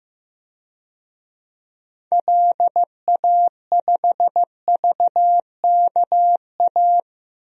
Morse code, heard as a keyed tone, about 15 wpm, 700 Hz. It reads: LA5VKA
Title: DXCC-100 Call Signs - 6 Characters - 15wpm